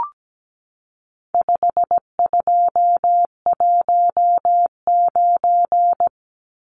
Morse code, heard as a keyed tone, about 17 wpm, 700 Hz. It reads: 5219
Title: Numbers - 4-Digit 17wpm